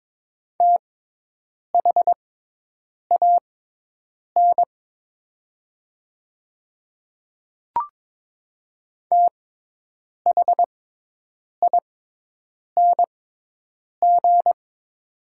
Morse code, heard as a keyed tone, about 22 wpm, spaced as for 8 wpm, 700 Hz. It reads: THAN THING
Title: 6x Character Spacing - Top 300 Words - 22wpm